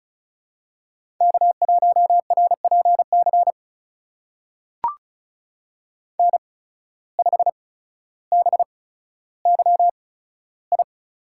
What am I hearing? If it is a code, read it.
Morse code, heard as a keyed tone, about 36 wpm, spaced as for 15 wpm, 700 Hz. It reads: K1RPC N5BYI
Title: USA Call Signs - 5 Characters - 8x Character Spacing - 35wpm